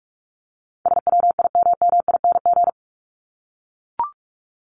Morse code, heard as a keyed tone, about 44 wpm, 700 Hz. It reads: SWIMMING
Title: Top 1501-1750 Words 45wpm